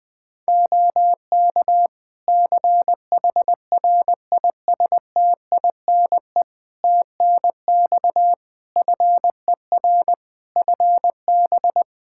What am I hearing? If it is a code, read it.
Morse code, heard as a keyed tone, about 20 wpm, 700 Hz. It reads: OK CHRISTINE TNX FER FB
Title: Practice QSO - Exchange 4 - Ending - 20wpm